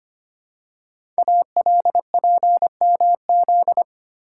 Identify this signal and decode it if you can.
Morse code, heard as a keyed tone, about 25 wpm, 700 Hz. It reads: ALPMZ
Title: Code Groups - 5 Letter:Numbers - 25wpm